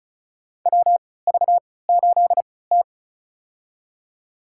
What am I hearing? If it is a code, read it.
Morse code, heard as a keyed tone, about 36 wpm, spaced as for 20 wpm, 700 Hz. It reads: WV8T
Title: USA Call Signs - 4 Characters - 3x Character Spacing - 35wpm